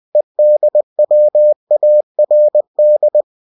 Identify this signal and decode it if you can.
Morse code, heard as a keyed tone, about 20 wpm, 600 Hz. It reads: EDWARD